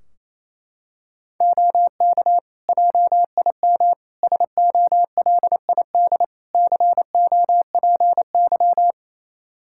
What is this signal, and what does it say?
Morse code, heard as a keyed tone, about 28 wpm, 700 Hz. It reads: OK JIM SOLID COPY
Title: Practice QSO - Exchange 3 - More Info - 28wpm